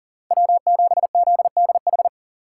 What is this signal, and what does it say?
Morse code, heard as a keyed tone, about 50 wpm, 700 Hz. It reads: W7ZDH
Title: Speed Racing - USA Call Signs 40wpm